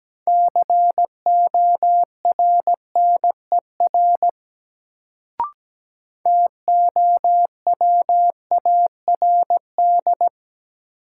Morse code, heard as a keyed tone, about 17 wpm, 700 Hz. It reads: CORNER TOWARD
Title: Sets of 5 Words - New Word Drill - 17wpm